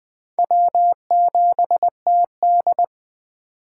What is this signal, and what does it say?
Morse code, heard as a keyed tone, about 24 wpm, 700 Hz. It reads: W7TD
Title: Speed Racing - USA 1x2 Call Signs 20wpm